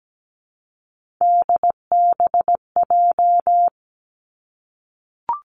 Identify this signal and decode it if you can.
Morse code, heard as a keyed tone, about 17 wpm, 700 Hz. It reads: DBJ
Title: Three Challenging Letter-Number Combinations 17wpm